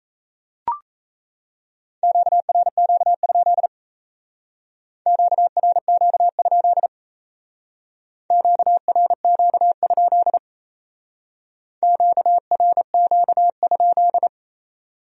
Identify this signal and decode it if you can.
Morse code, heard as a keyed tone, about 36 wpm, 700 Hz. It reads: QRQ? QRQ? QRQ? QRQ?